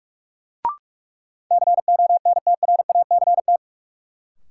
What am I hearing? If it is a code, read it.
Morse code, heard as a keyed tone, about 48 wpm, 700 Hz. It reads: CONTRACT
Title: Sentences from Top 1400 Words - New Word Drill Alt - 45wpm